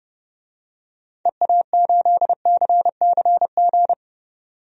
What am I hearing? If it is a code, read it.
Morse code, heard as a keyed tone, about 30 wpm, 700 Hz. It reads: EA8CCG
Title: DXCC-100 Call Signs - 6 Characters - 30wpm